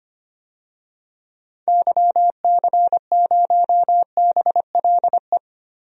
Morse code, heard as a keyed tone, about 25 wpm, 700 Hz. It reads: YC0BLE
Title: DXCC-100 Call Signs - 6 Characters - Repeated 3x - 25wpm